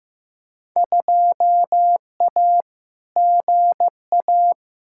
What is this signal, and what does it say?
Morse code, heard as a keyed tone, about 15 wpm, 700 Hz. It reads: 2A GA
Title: Field Day Exchanges Encore - 15wpm